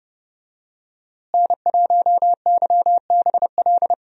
Morse code, heard as a keyed tone, about 30 wpm, 700 Hz. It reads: N1YBL
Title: USA Call Signs - 5 Characters - Repeated 2x - 30wpm